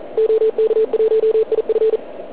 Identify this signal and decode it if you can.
Morse code, heard as a keyed tone, about 41 wpm, 440 Hz. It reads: OK1IF